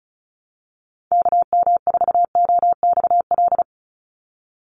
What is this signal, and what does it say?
Morse code, heard as a keyed tone, about 34 wpm, 700 Hz. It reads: KM4OXL
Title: USA Call Signs 35wpm